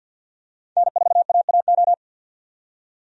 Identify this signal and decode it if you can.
Morse code, heard as a keyed tone, about 52 wpm, 700 Hz. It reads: N4AAO
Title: USA Call Signs - 5 Characters - 50wpm